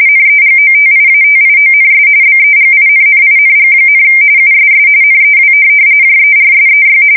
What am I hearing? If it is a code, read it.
RTTY, about 45.45 baud. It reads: NEGOTIATIONS BEGINNING, USS NEW JERSEY
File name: fsk170 / USS-New-Jersey